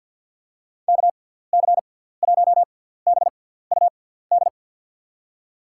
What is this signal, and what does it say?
Morse code, heard as a keyed tone, about 50 wpm, spaced as for 17 wpm, 700 Hz. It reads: KC1BUD
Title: USA Call Signs - 6 Characters - 6x Character Spacing - 50wpm